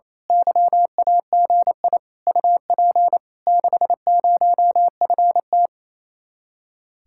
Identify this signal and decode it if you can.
Morse code, heard as a keyed tone, about 27 wpm, 700 Hz. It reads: YAGI UP 60FT
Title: Speed-Racing Common QSO Phrases - 28wpm